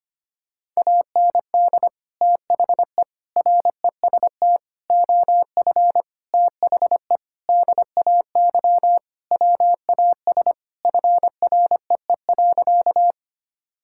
Morse code, heard as a keyed tone, about 25 wpm, 700 Hz. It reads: AND THE REST OF THE DAY WAS FREE.
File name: practice / 25wpm / Sentences with 4-Word Phrases - 25wpm